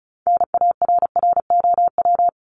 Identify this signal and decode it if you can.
Morse code, heard as a keyed tone, about 34 wpm, 700 Hz. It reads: NARROW
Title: Top 1501-1750 Words 35wpm